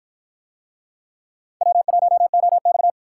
Morse code, heard as a keyed tone, about 48 wpm, 700 Hz. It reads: W1OX